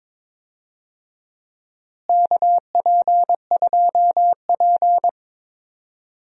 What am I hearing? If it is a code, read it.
Morse code, heard as a keyed tone, about 22 wpm, 700 Hz. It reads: KP2P
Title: DXCC-100 Call Signs - 4 Characters - Repeated 2x - 22wpm